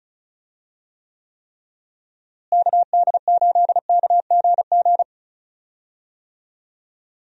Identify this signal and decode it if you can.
Morse code, heard as a keyed tone, about 34 wpm, 700 Hz. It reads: KD8KGG
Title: USA Call Signs - 6 Characters - Repeated 3x - 35wpm